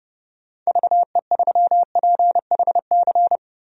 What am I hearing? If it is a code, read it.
Morse code, heard as a keyed tone, about 30 wpm, 700 Hz. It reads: VE3PHC
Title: DXCC-100 Call Signs - 6 Characters - 30wpm